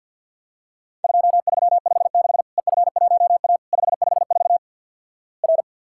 Morse code, heard as a keyed tone, about 50 wpm, 700 Hz. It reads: 2356 EF1A 554 R
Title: Worked All Europe DX Contest - QTC Exchange with Repeats - 50wpm